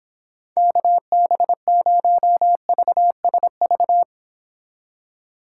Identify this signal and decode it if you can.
Morse code, heard as a keyed tone, about 30 wpm, 700 Hz. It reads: KB0VSV